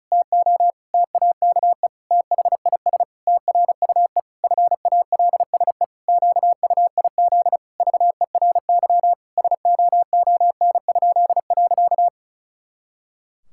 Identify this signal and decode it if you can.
Morse code, heard as a keyed tone, about 36 wpm, 700 Hz. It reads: TO TAKE THIS TRUE FALSE QUIZ VERY SOON?.
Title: Pangram Challenge v2 - 35wpm